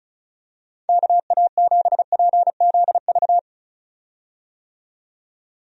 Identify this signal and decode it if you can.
Morse code, heard as a keyed tone, about 36 wpm, 700 Hz. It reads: KA7PZV
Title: USA Call Signs - 6 Characters - 35wpm